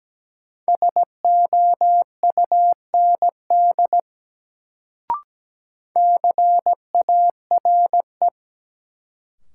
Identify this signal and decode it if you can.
Morse code, heard as a keyed tone, about 17 wpm, 700 Hz. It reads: SOUND CARE
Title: Sets of 3 Words - New Word Drill - 17wpm